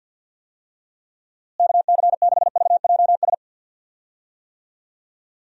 Morse code, heard as a keyed tone, about 52 wpm, 700 Hz. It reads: KC6VJS
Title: USA Call Signs - 6 Characters - Repeated 2x - 50wpm